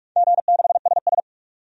Morse code, heard as a keyed tone, about 47 wpm, 700 Hz. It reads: G6SS
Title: DXCC-100 Call Signs - 4 Characters - 45wpm